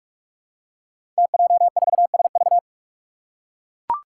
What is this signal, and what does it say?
Morse code, heard as a keyed tone, about 44 wpm, 700 Hz. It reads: TJ4SV
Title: Five Challenging Letter-Number Combinations 45wpm